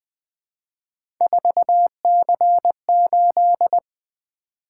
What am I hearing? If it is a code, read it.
Morse code, heard as a keyed tone, about 20 wpm, 700 Hz. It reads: 4C8
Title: Three Letter-Number Combinations - 20wpm